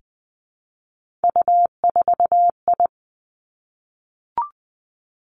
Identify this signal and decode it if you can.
Morse code, heard as a keyed tone, about 20 wpm, 700 Hz. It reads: U4I
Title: Three Challenging Letter-Number Combinations 20wpm